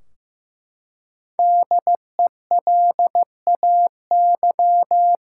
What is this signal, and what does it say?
Morse code, heard as a keyed tone, about 15 wpm, 700 Hz. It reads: DELAY
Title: Five Letter Words - 15wpm